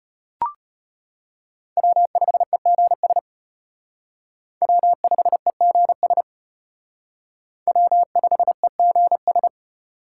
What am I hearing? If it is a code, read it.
Morse code, heard as a keyed tone, about 31 wpm, 700 Hz. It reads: W5EGS W5EGS W5EGS